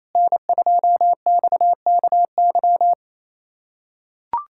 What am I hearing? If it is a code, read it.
Morse code, heard as a keyed tone, about 28 wpm, 700 Hz. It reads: N2XKY
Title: USA Callsign DE Callsign - 5 Characters - 28wpm